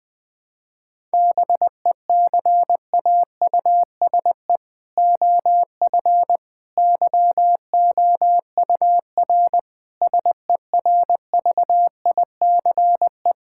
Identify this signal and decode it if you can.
Morse code, heard as a keyed tone, about 20 wpm, 700 Hz. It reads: BECAUSE OF YOUR SERVICE
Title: Sentences from Top 1000 Words - Review Part 5 - 20wpm